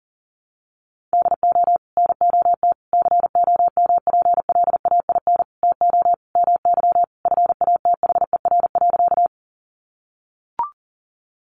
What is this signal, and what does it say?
Morse code, heard as a keyed tone, about 40 wpm, 700 Hz. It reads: DO NOT COMPLAIN TO MY FATHER.
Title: Sentences from Top 1300 Words 40wpm